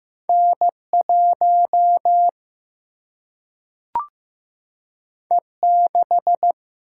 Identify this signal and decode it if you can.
Morse code, heard as a keyed tone, about 15 wpm, 700 Hz. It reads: N1 E6
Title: Single-Letter Single-Number 15wpm